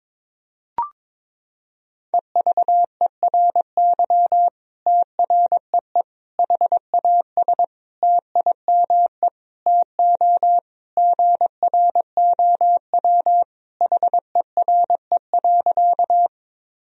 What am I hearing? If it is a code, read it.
Morse code, heard as a keyed tone, about 22 wpm, 700 Hz. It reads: EVERY TREE HAS TIME TO GROW HERE.
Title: Sentences from Top 1000 Words - Review Part 1 - 22wpm